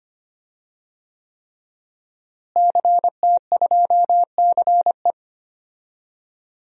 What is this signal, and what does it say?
Morse code, heard as a keyed tone, about 25 wpm, 700 Hz. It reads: CT2CE
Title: DXCC-100 Call Signs - 5 Characters - Repeated 3x - 25wpm